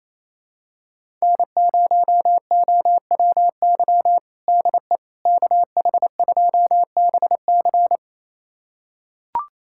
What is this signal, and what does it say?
Morse code, heard as a keyed tone, about 28 wpm, 700 Hz. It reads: N0OWY DE KH2BC
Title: USA Callsign DE Callsign - 5 Characters - 28wpm